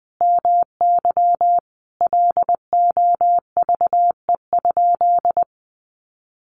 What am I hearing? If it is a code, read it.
Morse code, heard as a keyed tone, about 20 wpm, 700 Hz. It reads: MY LOVE?
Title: Sentences from Top 1500 Words 20wpm